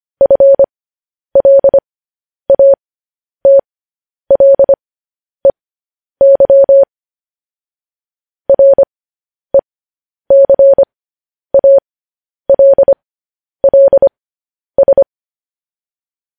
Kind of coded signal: Morse code